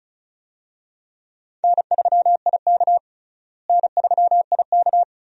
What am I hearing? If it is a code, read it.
Morse code, heard as a keyed tone, about 36 wpm, 700 Hz. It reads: N3IK N3IK
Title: USA Call Signs - 4 Characters - Repeated 3x - 35wpm